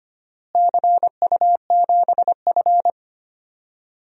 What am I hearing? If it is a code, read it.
Morse code, heard as a keyed tone, about 25 wpm, 700 Hz. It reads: CU7F
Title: DXCC-100 Call Signs - 4 Characters - Repeated 2x - 25wpm